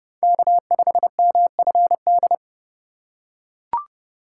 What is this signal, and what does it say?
Morse code, heard as a keyed tone, about 29 wpm, 700 Hz. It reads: K5MFD